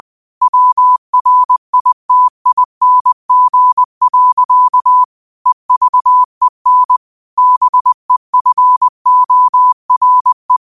Morse code, heard as a keyed tone, about 20 wpm, 1000 Hz. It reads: WRITING. EVEN BEFORE
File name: qotdfiles / 20wpm